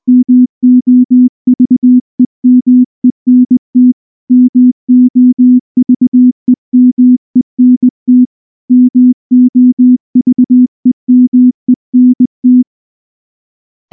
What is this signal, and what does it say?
Morse code, heard as a keyed tone, about 19 wpm, 250 Hz. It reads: MOVEMENT MOVEMENT MOVEMENT
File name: morse / morse-training-20WPM-1